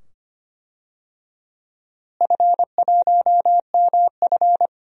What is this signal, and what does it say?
Morse code, heard as a keyed tone, about 25 wpm, 700 Hz. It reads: F1MF
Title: DXCC-100 Call Signs - 4 Characters - Repeated 2x - 25wpm